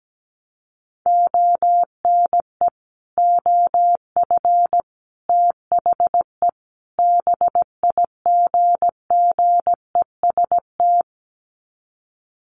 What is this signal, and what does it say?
Morse code, heard as a keyed tone, about 17 wpm, 700 Hz. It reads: ONE OF THE BIGGEST
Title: Sets of 4 Words 17wpm